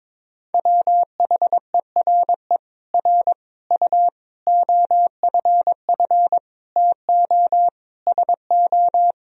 Morse code, heard as a keyed tone, about 22 wpm, 700 Hz. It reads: WHERE R U OFF TO SO
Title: Sentences from Top 400 Words and Abbreviations 22wpm